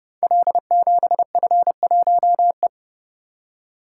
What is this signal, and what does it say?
Morse code, heard as a keyed tone, about 30 wpm, 700 Hz. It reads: L7F1E
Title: Code Groups - 5 Letter:Numbers - 30wpm